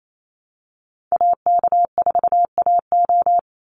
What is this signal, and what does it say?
Morse code, heard as a keyed tone, about 28 wpm, 690 Hz. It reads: AK4AO
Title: Call Signs Repeated 3x - 28wpm